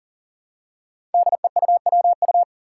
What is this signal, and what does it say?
Morse code, heard as a keyed tone, about 42 wpm, 700 Hz. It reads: DEUWU
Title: Five Challenging Letter-Number Combinations 40wpm